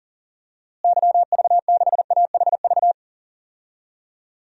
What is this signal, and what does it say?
Morse code, heard as a keyed tone, about 40 wpm, 700 Hz. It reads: YV6AHV
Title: DXCC-100 Call Signs - 6 Characters - Repeated 2x - 40wpm